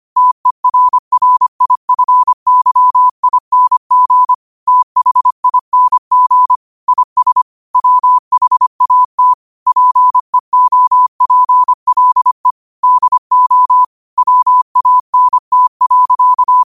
Morse code, heard as a keyed tone, about 25 wpm, 1000 Hz. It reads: TERRIFYING THING IS WHAT PEOPLE DO WANT.
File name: qotdfiles / 25wpm